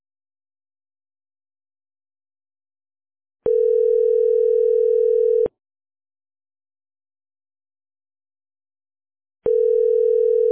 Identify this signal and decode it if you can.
Ringback tone (North America), 440 + 480 Hz.